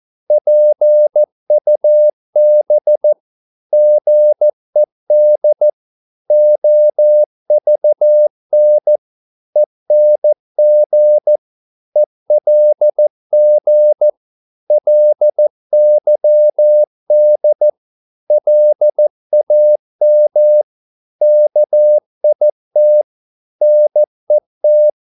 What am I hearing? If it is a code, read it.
Morse code, heard as a keyed tone, about 14 wpm, 600 Hz. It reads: PUB GED OVN ENG ELG LYD LAM KIT NET
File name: Korte ord DK 14wpm